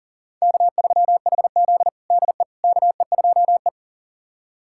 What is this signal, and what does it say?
Morse code, heard as a keyed tone, about 37 wpm, 700 Hz. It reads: K3HZ DE KE2E
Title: USA Callsign DE Callsign - 4 Characters - 40wpm